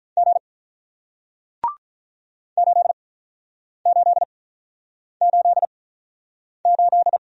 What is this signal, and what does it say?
Morse code, heard as a keyed tone, about 48 wpm, spaced as for 9 wpm, 700 Hz. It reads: G 8888